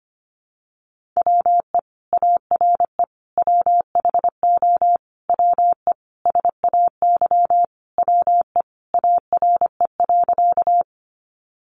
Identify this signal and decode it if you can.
Morse code, heard as a keyed tone, about 25 wpm, 700 Hz. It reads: WE ARE WHO WE SAY WE ARE.
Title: Sentences from Top 200 Words 25wpm